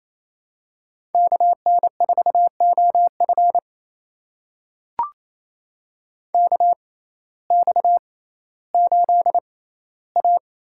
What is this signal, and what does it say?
Morse code, heard as a keyed tone, about 27 wpm, spaced as for 15 wpm, 700 Hz. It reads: KN4OF KX8A
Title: USA Call Signs - 6x Character Spacing - 28wpm